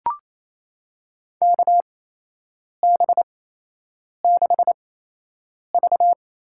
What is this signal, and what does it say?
Morse code, heard as a keyed tone, about 28 wpm, spaced as for 8 wpm, 700 Hz. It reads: KB6V
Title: USA Call Signs - 8x Character Spacing - 28wpm